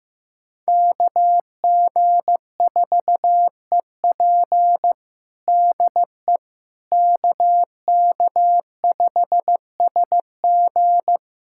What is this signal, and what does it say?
Morse code, heard as a keyed tone, about 15 wpm, 700 Hz. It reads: KG4EP DE KK5SG
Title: USA Callsign DE Callsign - 5 Characters - 15wpm